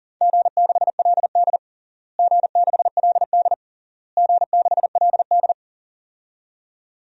Morse code, heard as a keyed tone, about 37 wpm, 700 Hz. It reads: G6LD G6LD G6LD
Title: DXCC-100 Call Signs - 4 Characters - Repeated 3x - 40wpm